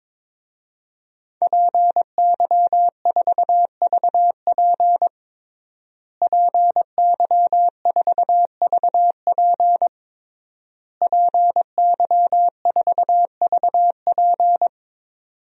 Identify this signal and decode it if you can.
Morse code, heard as a keyed tone, about 22 wpm, 700 Hz. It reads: PY4VP PY4VP PY4VP
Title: DXCC-100 Call Signs - 5 Characters - Repeated 3x - 22wpm